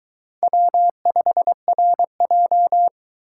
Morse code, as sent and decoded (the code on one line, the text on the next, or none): .-- ..... .-. .---
W5RJ